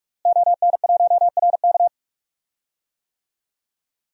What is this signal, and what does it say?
Morse code, heard as a keyed tone, about 41 wpm, 700 Hz. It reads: ON1RK
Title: DXCC-100 Call Signs - 5 Characters - Repeated 3x - 45wpm